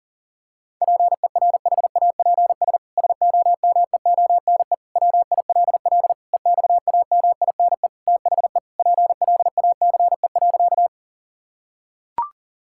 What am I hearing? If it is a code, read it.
Morse code, heard as a keyed tone, about 40 wpm, 700 Hz. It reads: PERHAPS SOMEONE WILL EXAMINE THE PLACE.
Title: Sentences from Top 1600 Words 40wpm